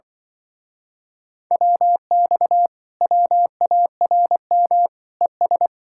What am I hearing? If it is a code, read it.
Morse code, heard as a keyed tone, about 24 wpm, 700 Hz. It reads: WX WARM ES
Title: Speed-Racing Common QSO Phrases - 20wpm